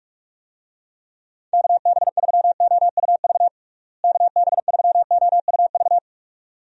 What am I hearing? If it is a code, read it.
Morse code, heard as a keyed tone, about 41 wpm, 700 Hz. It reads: KB3OUV KB3OUV
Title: USA Call Signs - 6 Characters - Repeated 2x - 45wpm